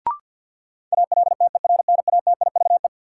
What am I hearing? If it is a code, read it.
Morse code, heard as a keyed tone, about 51 wpm, 700 Hz. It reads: ALTERNATIVE
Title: Sentences from Top 1000 Words - Review Part 4 - New Word Drill - 50wpm